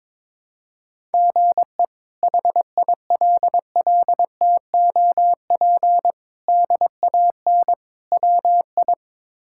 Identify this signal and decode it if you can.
Morse code, heard as a keyed tone, about 22 wpm, 700 Hz. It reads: GE HILLTOP DAN WI
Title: K1USN SST QSO Practice - 22wpm